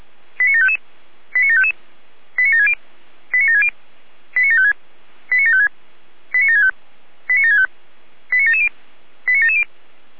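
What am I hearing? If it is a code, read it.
Five-tone selective call (ZVEI), heard as quick runs of five tones, tones of about 80 ms. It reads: 89755 89755 89760 89760 89756 89756 89765 89765 89770 89770
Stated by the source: Five-tone ZVEI selective call received on an SDR.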